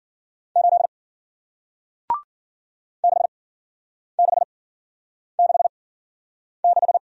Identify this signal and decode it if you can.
Morse code, heard as a keyed tone, about 53 wpm, spaced as for 9 wpm, 700 Hz. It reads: 8 6666